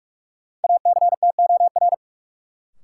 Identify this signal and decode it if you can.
Morse code, heard as a keyed tone, about 41 wpm, 700 Hz. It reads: ACTOR
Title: Sentences from Top 1500 Words - New Word Drill - 45wpm